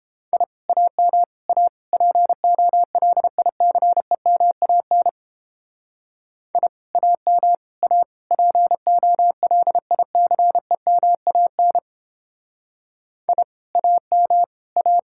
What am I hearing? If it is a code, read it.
Morse code, heard as a keyed tone, about 29 wpm, 700 Hz. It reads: I AM A POLICEMAN I AM A POLICEMAN I AM A